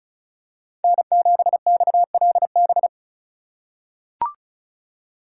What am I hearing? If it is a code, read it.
Morse code, heard as a keyed tone, about 36 wpm, 700 Hz. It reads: N7XLB
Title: USA Call Signs - 5 Characters - 3x Character Spacing - 35wpm